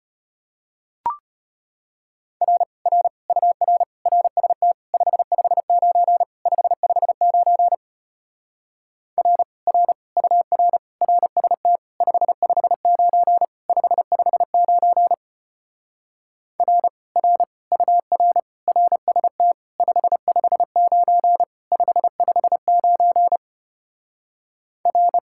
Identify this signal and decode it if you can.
Morse code, heard as a keyed tone, about 33 wpm, 700 Hz. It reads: R R UR RST 559 559 R R UR RST 559 559 R R UR RST 559 559 R